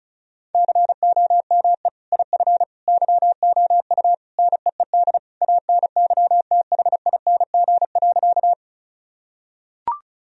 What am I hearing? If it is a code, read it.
Morse code, heard as a keyed tone, about 34 wpm, 700 Hz. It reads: COME IF YOU NEED ANYTHING.
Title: Sentences from Top 1000 Words - Review Part 1 - 35wpm